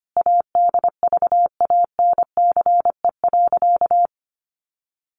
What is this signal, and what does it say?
Morse code, heard as a keyed tone, about 25 wpm, 700 Hz. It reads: ADVANCE.
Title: Sentences from Top 1400 Words 25wpm